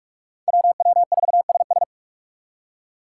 Morse code, heard as a keyed tone, about 58 wpm, 700 Hz. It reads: WW4SS